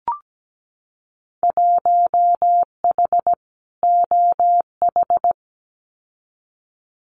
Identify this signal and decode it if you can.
Morse code, heard as a keyed tone, about 17 wpm, 700 Hz. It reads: 1H OH
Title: Winter Field Day Exchanges 17wpm